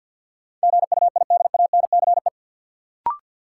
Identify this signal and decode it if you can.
Morse code, heard as a keyed tone, about 52 wpm, 700 Hz. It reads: GUIDANCE
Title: Sentences from Top 1600 Words - New Word Drill - 50wpm